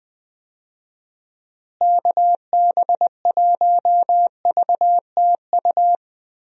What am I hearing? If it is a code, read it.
Morse code, heard as a keyed tone, about 21 wpm, 700 Hz. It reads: KB1VTU
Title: USA Call Signs - 6 Characters - Repeated 3x - 20wpm